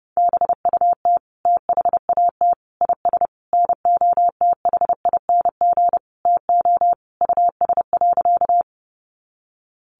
Morse code, heard as a keyed tone, about 30 wpm, 700 Hz. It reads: BUT THAT IS NOTHING TO US.
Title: Sentences from Top 500 Words 30wpm